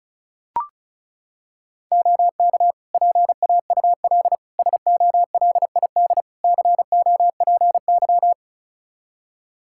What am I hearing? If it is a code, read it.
Morse code, heard as a keyed tone, about 36 wpm, 700 Hz. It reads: OK PAUL SOLID COPY
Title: K1USN SST QSO Practice 35wpm